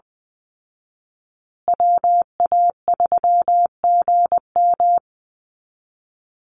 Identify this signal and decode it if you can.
Morse code, heard as a keyed tone, about 20 wpm, 700 Hz. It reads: WA3GM
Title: Call Signs Repeated 2x - 20wpm